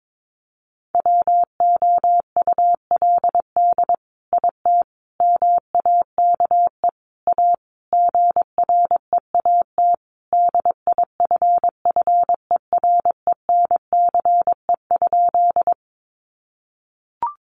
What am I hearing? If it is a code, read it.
Morse code, heard as a keyed tone, about 22 wpm, 700 Hz. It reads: WOULD IT MAKE A GREAT DIFFERENCE?
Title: Sentences from Top 800 Words 22wpm